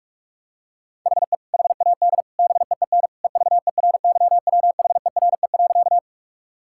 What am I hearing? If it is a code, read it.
Morse code, heard as a keyed tone, about 46 wpm, 700 Hz. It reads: HE HAD BEEN EVERYWHERE.
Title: Sentences from Top 1600 Words 45wpm